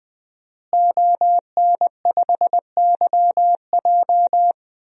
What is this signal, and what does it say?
Morse code, heard as a keyed tone, about 19 wpm, 700 Hz. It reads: ON5YJ
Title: DXCC-100 Call Signs - 5 Characters - Repeated 3x - 20wpm